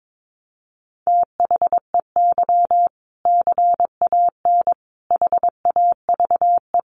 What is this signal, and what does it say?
Morse code, heard as a keyed tone, about 22 wpm, 700 Hz. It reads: THEY CAN HAVE
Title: Sentences from Top 400 Words 22wpm